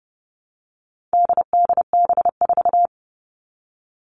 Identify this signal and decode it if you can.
Morse code, heard as a keyed tone, about 30 wpm, 700 Hz. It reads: DDB4
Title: All Cut Numbers - 4-Digit 30wpm